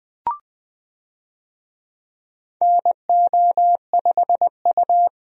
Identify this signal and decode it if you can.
Morse code, heard as a keyed tone, about 20 wpm, 700 Hz. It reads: NO5U
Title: USA Call Signs - 4 Characters - 20wpm